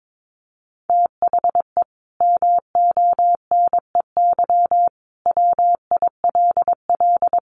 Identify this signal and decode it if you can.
Morse code, heard as a keyed tone, about 22 wpm, 700 Hz. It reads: THE MONEY WILL
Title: Sentences from Top 1500 Words 22wpm